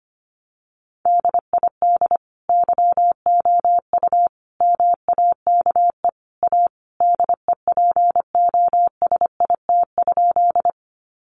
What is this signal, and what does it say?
Morse code, heard as a keyed tone, about 25 wpm, 700 Hz. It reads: DID YOU MAKE A DEPOSIT?
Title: Sentences from Top 1300 Words 25wpm